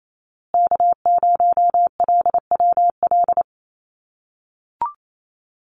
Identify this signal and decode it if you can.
Morse code, heard as a keyed tone, about 28 wpm, 700 Hz. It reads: K0LWL
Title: USA Call Signs Repeated 3x 28wpm